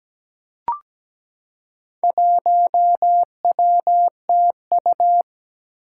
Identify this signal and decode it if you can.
Morse code, heard as a keyed tone, about 17 wpm, 700 Hz. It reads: 1WTU
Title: Four Challenging Letter-Number Combinations 17wpm